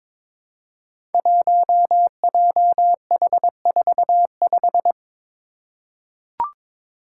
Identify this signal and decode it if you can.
Morse code, heard as a keyed tone, about 22 wpm, 700 Hz. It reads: 1JH45
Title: Five Challenging Letter-Number Combinations 22wpm